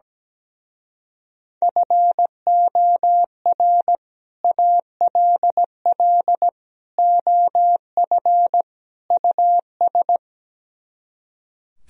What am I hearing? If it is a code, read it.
Morse code, heard as a keyed tone, about 17 wpm, 700 Hz. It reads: FOR ALL OF US
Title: Speed-Racing - Sets of 4 Words 17wpm